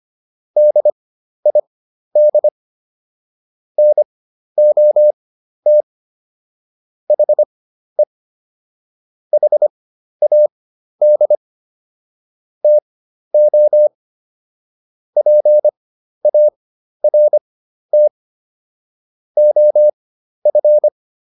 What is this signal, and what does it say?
Morse code, heard as a keyed tone, about 25 wpm, spaced as for 12 wpm, 600 Hz. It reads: DID NOT HE HAD TO PART OF